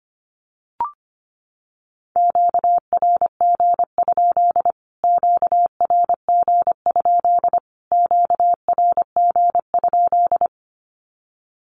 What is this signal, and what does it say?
Morse code, heard as a keyed tone, about 25 wpm, 700 Hz. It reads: QRG? QRG? QRG?
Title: Q-Codes Repeated 3x 25wpm